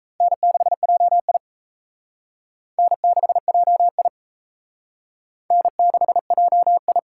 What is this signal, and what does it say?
Morse code, heard as a keyed tone, about 35 wpm, 700 Hz. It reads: N6JI N6JI N6JI
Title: Speed Racing - USA 1x2 Call Signs 28wpm